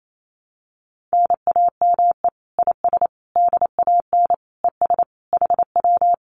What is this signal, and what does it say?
Morse code, heard as a keyed tone, about 28 wpm, 700 Hz. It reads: NAME IS DAN ES HW
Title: Common QSO Phrases Encore 28wpm